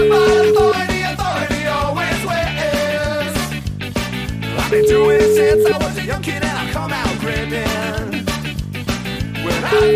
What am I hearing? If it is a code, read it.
Ringback tone (Europe, 425 Hz), with noise.